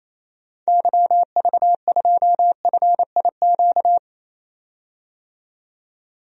Morse code, heard as a keyed tone, about 27 wpm, 700 Hz. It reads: YV2FIQ
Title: DXCC-100 Call Signs - 6 Characters - 28wpm